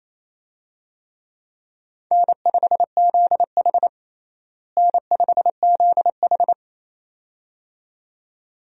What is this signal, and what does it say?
Morse code, heard as a keyed tone, about 28 wpm, 700 Hz. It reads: N5ZH N5ZH
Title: USA Call Signs - 4 Characters - Repeated 2x - 28wpm